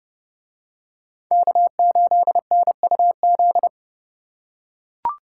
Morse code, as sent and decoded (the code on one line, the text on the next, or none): -.- ---.. -. ..- --..
K8NUZ